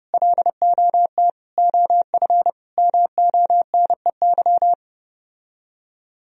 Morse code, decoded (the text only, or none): LOT OF MONEY